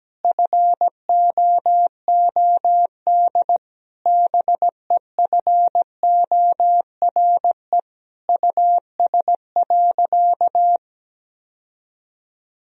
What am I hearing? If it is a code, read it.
Morse code, heard as a keyed tone, about 17 wpm, 700 Hz. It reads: FOOD BEFORE US.
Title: Sentences from Top 1000 Words - Review Part 4 - 17wpm